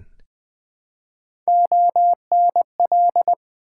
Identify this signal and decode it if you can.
Morse code, heard as a keyed tone, about 20 wpm, 700 Hz. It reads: ONL
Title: Build A Word - 1 Letter At A Time - 3 to 5-Letter Words - 20wpm